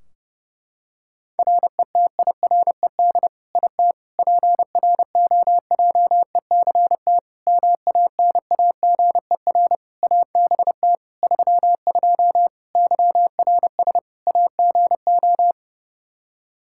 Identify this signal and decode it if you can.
Morse code, heard as a keyed tone, about 32 wpm, 700 Hz. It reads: RETIRED IT PROJECT MANAGER ABT 32 YRS AGO
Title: Practice QSO - Exchange 3 - More Info - 30wpm